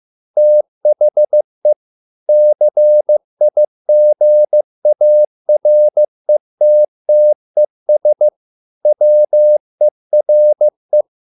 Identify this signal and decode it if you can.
Morse code, heard as a keyed tone, about 15 wpm, 600 Hz. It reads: THE CIGARETTES WERE